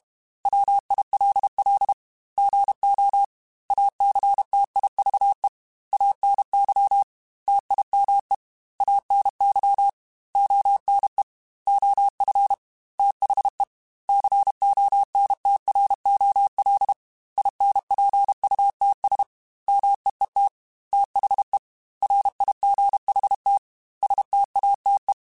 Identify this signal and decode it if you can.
Morse code, heard as a keyed tone, about 32 wpm, 800 Hz. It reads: WILL GO ACTIVE ANY TIME ANY ONE OF THE CONTROL INPUTS MEET THE RIGHT STATE
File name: Technical Subjects / Technical Subjects-mp3s / OR Gate @30